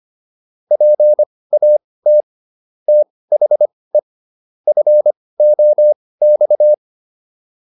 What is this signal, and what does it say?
Morse code, heard as a keyed tone, about 25 wpm, spaced as for 18 wpm, 600 Hz. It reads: PAT THE FOX